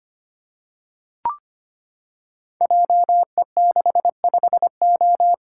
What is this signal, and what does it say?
Morse code, heard as a keyed tone, about 25 wpm, 700 Hz. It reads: JE65O
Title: Code Groups - 5 Letter:Numbers - 25wpm